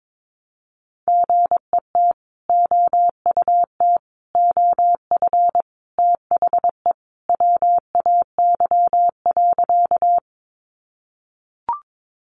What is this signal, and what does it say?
Morse code, heard as a keyed tone, about 22 wpm, 700 Hz. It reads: GET OUT OF THE WAY.
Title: Sentences from Top 200 Words 22wpm